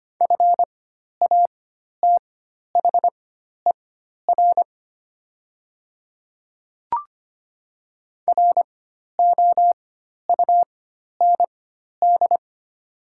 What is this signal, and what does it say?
Morse code, heard as a keyed tone, about 25 wpm, spaced as for 12 wpm, 700 Hz. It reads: FATHER ROUND
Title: 4x Character Spacing - Top 200 Words - 25wpm